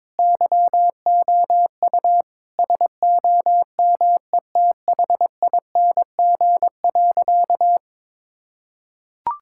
Morse code, heard as a keyed tone, about 21 wpm, 700 Hz. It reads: YOU SOMETHING.
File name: practice / 22wpm / Sentences with 4-Word Phrases - 22wpm